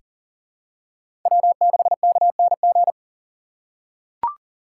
Morse code, heard as a keyed tone, about 40 wpm, 700 Hz. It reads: W6KNG
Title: USA Call Signs - 5 Characters - 3x Character Spacing - 40wpm